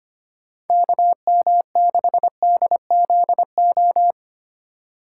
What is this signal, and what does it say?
Morse code, heard as a keyed tone, about 25 wpm, 700 Hz. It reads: KM6DZO
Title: USA Call Signs - 6 Characters - 3x Character Spacing - 25wpm